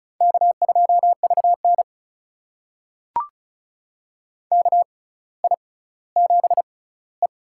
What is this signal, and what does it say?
Morse code, heard as a keyed tone, about 36 wpm, spaced as for 18 wpm, 700 Hz. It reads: K2VN KI7E
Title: USA Call Signs - 6x Character Spacing - 35wpm